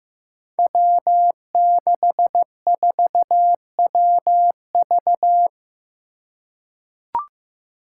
Morse code, heard as a keyed tone, about 15 wpm, 700 Hz. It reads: W64WV
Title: Five Challenging Letter-Number Combinations 15wpm